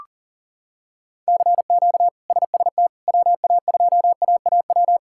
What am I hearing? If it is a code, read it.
Morse code, heard as a keyed tone, about 40 wpm, 700 Hz. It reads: CQ SST WA2AAW
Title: K1USN SST QSO Practice - 40wpm